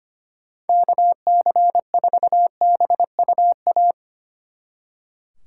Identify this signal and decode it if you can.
Morse code, heard as a keyed tone, about 27 wpm, 700 Hz. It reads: KC4BUA
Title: Speed Racing - USA 2x3 Call Signs 25wpm